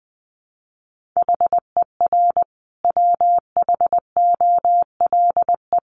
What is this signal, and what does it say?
Morse code, heard as a keyed tone, about 20 wpm, 700 Hz. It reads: HER WHOLE
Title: Sentences from Top 1500 Words 20wpm